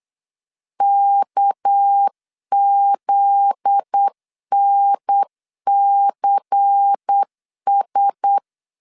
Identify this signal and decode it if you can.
Morse code, heard as a keyed tone, about 8 wpm, 800 Hz. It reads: KZNCS
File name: LEZIONI / 7 GF9Z / 7 GF9Z 8wpm 3213